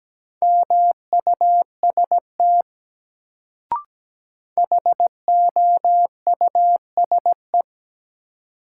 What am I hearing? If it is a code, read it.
Morse code, heard as a keyed tone, about 17 wpm, 700 Hz. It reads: MUST HOUSE
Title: Sentences from Top 200 Words - New Word Drill - 17wpm